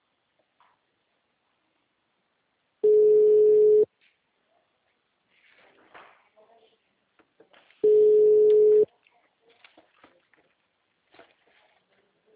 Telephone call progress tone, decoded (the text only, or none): ringback tone